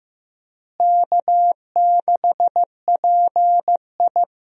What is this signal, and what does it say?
Morse code, heard as a keyed tone, about 15 wpm, 700 Hz. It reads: K6PI
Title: USA Call Signs - 4 Characters - Repeated 2x - 15wpm